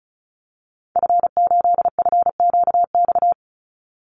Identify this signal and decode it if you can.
Morse code, heard as a keyed tone, about 34 wpm, 700 Hz. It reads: F8FQX
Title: Call Signs Repeated 3x - 35wpm